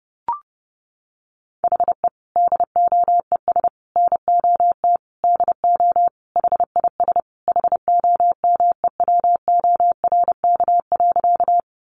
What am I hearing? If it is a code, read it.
Morse code, heard as a keyed tone, about 30 wpm, 700 Hz. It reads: HE DOES NOT DO HIS HOMEWORK.
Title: Additional Easy Phrases 30wpm